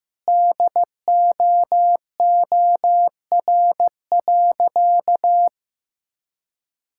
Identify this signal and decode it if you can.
Morse code, heard as a keyed tone, about 15 wpm, 700 Hz. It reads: DOOR.
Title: Sentences from Top 1000 Words - Review Part 6 - 15wpm